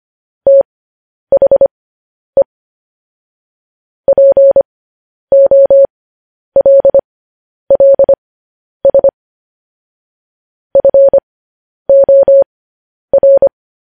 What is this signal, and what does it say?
Morse code, heard as a keyed tone, about 25 wpm, spaced as for 10 wpm, 550 Hz. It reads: THE POLLS FOR
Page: news-headlines-in-morse-code-at-10-wpm